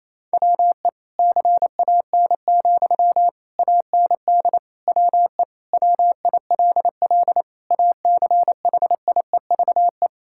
Morse code, heard as a keyed tone, about 28 wpm, 700 Hz. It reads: WE CAN, AND WE WILL ACHIEVE
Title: Sentences from Top 1000 Words - Review Part 6 - 28wpm